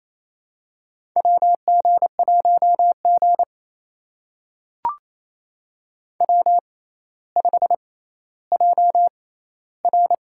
Morse code, heard as a keyed tone, about 28 wpm, spaced as for 14 wpm, 700 Hz. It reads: WG1G W5JR
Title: USA Call Signs - 4 Characters - 6x Character Spacing - 28wpm